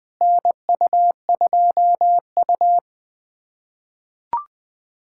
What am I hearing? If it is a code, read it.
Morse code, heard as a keyed tone, about 21 wpm, 700 Hz. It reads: NU2U
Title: USA Callsign DE Callsign - 4 Characters - 20wpm